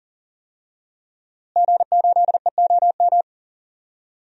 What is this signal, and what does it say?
Morse code, heard as a keyed tone, about 39 wpm, 700 Hz. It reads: G8EOM
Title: DXCC-100 Call Signs - 5 Characters - Repeated 2x - 40wpm